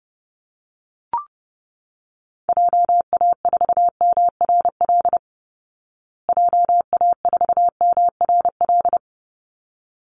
Morse code, heard as a keyed tone, about 30 wpm, 700 Hz. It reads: JA4MRL JA4MRL
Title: Call Signs Repeated 3x - 30wpm